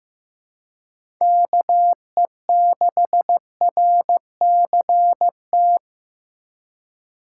Morse code, heard as a keyed tone, about 15 wpm, 700 Hz. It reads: KE6RCT
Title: USA Call Signs - 2x Character Spacing - 15wpm